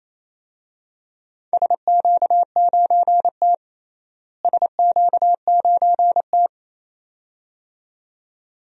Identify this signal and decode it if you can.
Morse code, heard as a keyed tone, about 28 wpm, 700 Hz. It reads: SQ9T SQ9T
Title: DXCC-100 Call Signs - 4 Characters - Repeated 2x - 28wpm